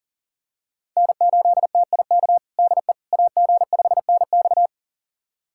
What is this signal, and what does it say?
Morse code, heard as a keyed tone, about 40 wpm, 700 Hz. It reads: N8TIK DE AG5NX
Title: USA Callsign DE Callsign - 5 Characters - 40wpm